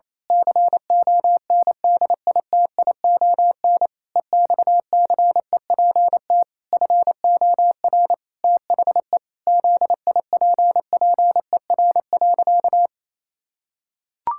Morse code, heard as a keyed tone, about 28 wpm, 700 Hz. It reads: CONDITION EXCEPT FOR THE ZIPPER.
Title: Pangram Challenge v2 - 28wpm